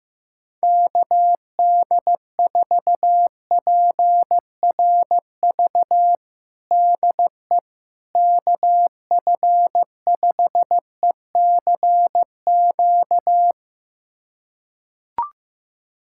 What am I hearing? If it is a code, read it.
Morse code, heard as a keyed tone, about 15 wpm, 700 Hz. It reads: KD4PRV DE KF5ECQ
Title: USA Callsign DE Callsign - 6 Characters - 15wpm